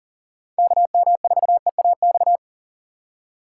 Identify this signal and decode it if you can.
Morse code, heard as a keyed tone, about 40 wpm, 700 Hz. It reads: KM4EAX
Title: USA Call Signs - 2x Character Spacing - 40wpm